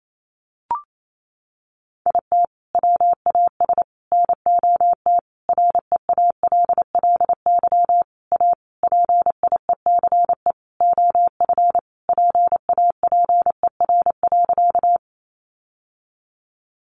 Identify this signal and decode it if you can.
Morse code, heard as a keyed tone, about 28 wpm, 700 Hz. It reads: IT WAS NOT REALLY A PIECE OF PAPER.
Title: Sentences from Top 600 Words 28wpm